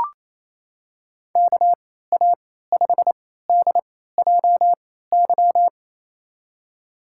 Morse code, heard as a keyed tone, about 28 wpm, spaced as for 16 wpm, 700 Hz. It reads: KA5DJY
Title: USA Call Signs - 6 Characters - 3x Character Spacing - 28wpm